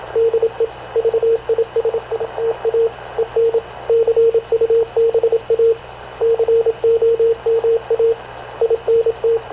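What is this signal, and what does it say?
Morse code, heard as a keyed tone, about 26 wpm, 460 Hz, with radio noise. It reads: DE VISITA EN CUBA COMA INT
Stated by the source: Havana Radio CLA press (PX) broadcast, recorded by Radio Officer Paul Dumesnil aboard mv CNAV QUEST/CZDO.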